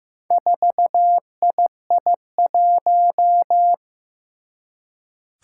Morse code, heard as a keyed tone, about 15 wpm, 700 Hz. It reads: 4II1
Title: Four Challenging Letter-Number Combinations 15wpm